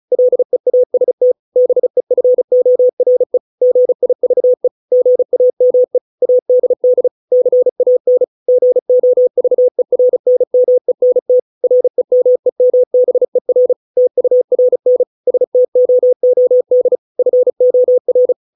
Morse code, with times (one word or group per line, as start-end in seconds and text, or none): 0.1-1.3 LEAST
1.6-3.4 BEFORE
3.6-4.7 GIVE
4.9-6.0 GAME
6.2-7.1 ADD
7.3-8.2 CAN
8.5-11.4 GOVERNMENT
11.6-13.7 REMEMBER
14.0-15.0 TURN
15.3-17.0 STOOD
17.2-18.3 FOR